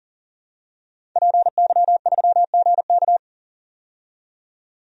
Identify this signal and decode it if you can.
Morse code, heard as a keyed tone, about 40 wpm, 700 Hz. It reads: PY3GK
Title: DXCC-100 Call Signs - 5 Characters - 40wpm